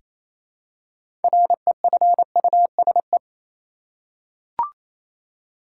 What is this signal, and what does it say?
Morse code, heard as a keyed tone, about 28 wpm, 700 Hz. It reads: REFUSE
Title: Six Letter Words - 28wpm